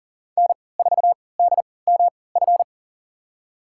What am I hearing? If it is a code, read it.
Morse code, heard as a keyed tone, about 40 wpm, spaced as for 23 wpm, 700 Hz. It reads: N4DMF
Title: USA Call Signs - 3x Character Spacing - 40wpm